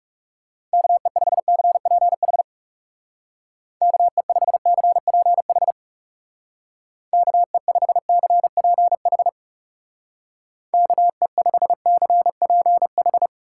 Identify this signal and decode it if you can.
Morse code, heard as a keyed tone, about 40 wpm, 700 Hz. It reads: KE5CPH KE5CPH KE5CPH KE5CPH